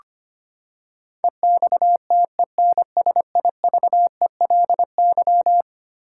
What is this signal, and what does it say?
Morse code, heard as a keyed tone, about 25 wpm, 700 Hz. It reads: EXTENSIVELY
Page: morse-code-ninja-practice-25wpm